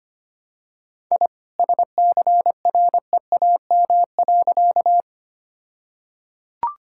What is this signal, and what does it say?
Morse code, heard as a keyed tone, about 25 wpm, 700 Hz. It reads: I SCREAM.
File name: practice / 25wpm / All-Age Appropriate Jokes - 25wpm